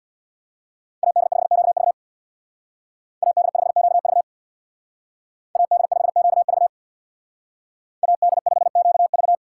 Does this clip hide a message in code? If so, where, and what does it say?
Yes: Morse code from 1.0 to 9.3 s, heard as a keyed tone, about 64 wpm, 700 Hz. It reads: AD5QV AD5QV AD5QV AD5QV